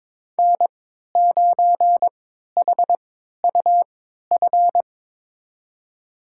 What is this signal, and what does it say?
Morse code, heard as a keyed tone, about 21 wpm, spaced as for 12 wpm, 700 Hz. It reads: N9HUF
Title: USA Call Signs - 5 Characters - 3x Character Spacing - 22wpm